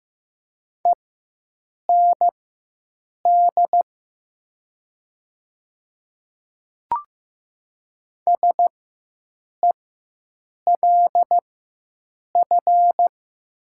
Morse code, heard as a keyed tone, about 15 wpm, spaced as for 7 wpm, 700 Hz. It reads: END SELF
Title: 4x Character Spacing - Top 200 Words - 15wpm